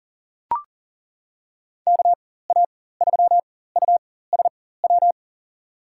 Morse code, heard as a keyed tone, about 38 wpm, spaced as for 18 wpm, 700 Hz. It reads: KA3USW
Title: USA Call Signs - 6 Characters - 4x Character Spacing - 40wpm